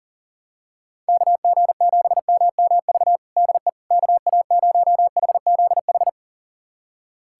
Morse code, heard as a keyed tone, about 40 wpm, 700 Hz. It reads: KG7MMV DE KA0HZH
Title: USA Callsign DE Callsign 40wpm